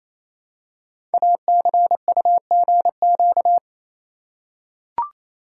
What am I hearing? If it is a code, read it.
Morse code, heard as a keyed tone, about 27 wpm, 700 Hz. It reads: ACUGQ
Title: Code Groups - 5 Letters - 28wpm